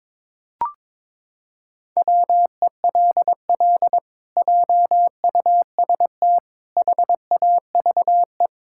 Morse code, heard as a keyed tone, about 22 wpm, 700 Hz. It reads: WELL JUST HAVE
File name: practice / 22wpm / Sets of 7 Words - 22wpm